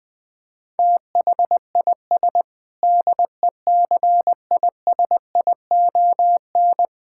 Morse code, heard as a keyed tone, about 20 wpm, 700 Hz. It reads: THIS DECISION
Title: Sentences with 7-Word Phrases - Spoken Beginning - 20wpm